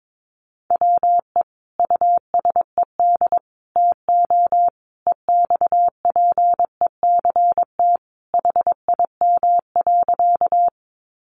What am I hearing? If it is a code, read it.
Morse code, heard as a keyed tone, about 22 wpm, 700 Hz. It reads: WE USED TO EXPECT HIM.
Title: Sentences from Top 1100 Words 22wpm